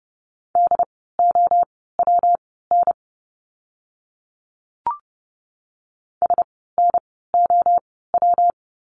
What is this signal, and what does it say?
Morse code, heard as a keyed tone, about 29 wpm, spaced as for 17 wpm, 700 Hz. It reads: DOWN SNOW
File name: course / 30:10wpm / 115 Lesson 19f - Letter w - Words 30wpm